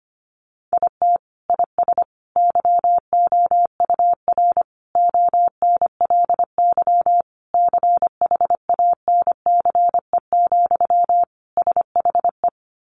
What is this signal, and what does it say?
Morse code, heard as a keyed tone, about 25 wpm, 700 Hz. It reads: IT IS YOUR ONLY CHANCE, SHE
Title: Sentences from Top 800 Words 25wpm